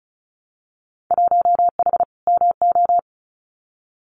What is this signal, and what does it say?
Morse code, heard as a keyed tone, about 36 wpm, 700 Hz. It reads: 1H MO
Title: Winter Field Day Exchanges 35wpm